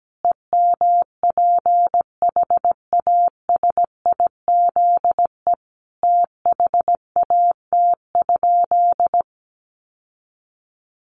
Morse code, heard as a keyed tone, about 17 wpm, 700 Hz. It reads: EMPHASIZE THAT?
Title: Sentences from Top 1500 Words 17wpm